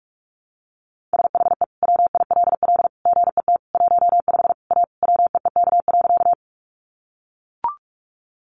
Morse code, heard as a keyed tone, about 44 wpm, 700 Hz. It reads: SHE WILL GET 15 A WEEK.
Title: Easy Phrases 45wpm